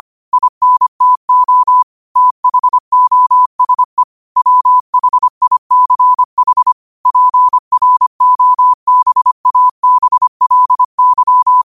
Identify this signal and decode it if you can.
Morse code, heard as a keyed tone, about 25 wpm, 1000 Hz. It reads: INTO THOSE WHICH PROBABLY
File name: qotdfiles / 25wpm